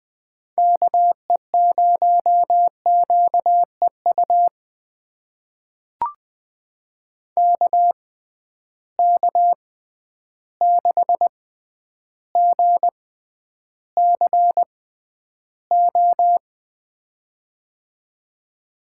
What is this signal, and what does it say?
Morse code, heard as a keyed tone, about 21 wpm, spaced as for 10 wpm, 700 Hz. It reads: KE0QEU KK6GCO
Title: USA Call Signs - 6 Characters - 6x Character Spacing - 20wpm